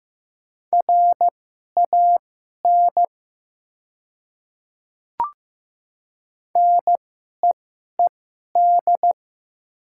Morse code, heard as a keyed tone, about 15 wpm, spaced as for 11 wpm, 700 Hz. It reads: RAN NEED
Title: 2x Character Spacing - Top 500 Words - 15wpm